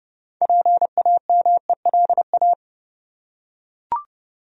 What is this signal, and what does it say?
Morse code, heard as a keyed tone, about 30 wpm, 700 Hz. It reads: PAMELA
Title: USA Common Names - 30wpm